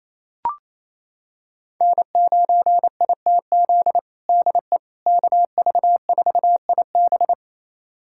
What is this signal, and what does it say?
Morse code, heard as a keyed tone, about 27 wpm, 700 Hz. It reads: N9ITZ DE KV4IB
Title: USA Callsign DE Callsign - 5 Characters - 28wpm